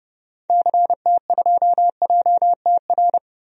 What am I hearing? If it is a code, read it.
Morse code, heard as a keyed tone, about 30 wpm, 700 Hz. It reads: CT2JTR
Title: DXCC-100 Call Signs - 6 Characters - Repeated 3x - 30wpm